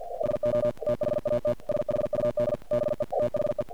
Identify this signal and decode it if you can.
Morse code, heard as a keyed tone, about 48 wpm, 590 Hz. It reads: SO THAT SHUD BE THE
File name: archived / CRSnet